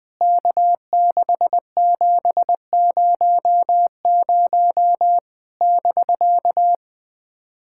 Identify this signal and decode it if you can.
Morse code, heard as a keyed tone, about 20 wpm, 700 Hz. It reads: K6700 <BK>
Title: POTA QSO Practice - 20wpm